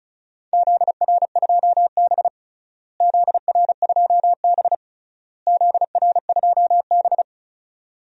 Morse code, heard as a keyed tone, about 34 wpm, 700 Hz. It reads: ZR2B ZR2B ZR2B
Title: DXCC-100 Call Signs - 4 Characters - Repeated 3x - 35wpm